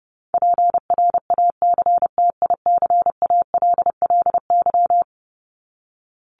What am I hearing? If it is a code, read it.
Morse code, heard as a keyed tone, about 30 wpm, 700 Hz. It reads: PRACTICALLY
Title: LY Suffix - Top 100 Words with Base Words 30wpm